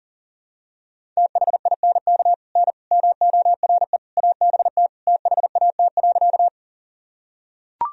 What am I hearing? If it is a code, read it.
Morse code, heard as a keyed tone, about 40 wpm, 700 Hz. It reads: THINK N MORE ABT THAT.
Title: Sentences from Top 200 Words and Abbreviations 40wpm